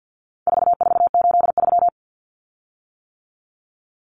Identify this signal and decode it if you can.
Morse code, heard as a keyed tone, about 50 wpm, 700 Hz. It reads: 4483
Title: Common Cut Numbers - 4-Digit 50wpm